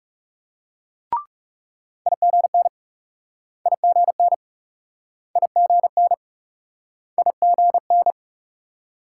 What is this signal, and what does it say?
Morse code, heard as a keyed tone, about 39 wpm, 700 Hz. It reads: IGN IGN IGN IGN